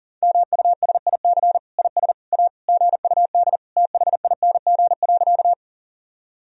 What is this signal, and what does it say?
Morse code, heard as a keyed tone, about 40 wpm, 700 Hz. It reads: MUSIC IS A GUD THING.
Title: Sentences from Top 300 Words and Abbreviations 40wpm